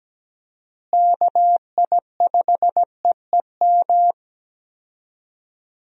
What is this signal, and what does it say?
Morse code, heard as a keyed tone, about 17 wpm, 700 Hz. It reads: KI5EEM
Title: USA Call Signs - 6 Characters - Repeated 2x - 17wpm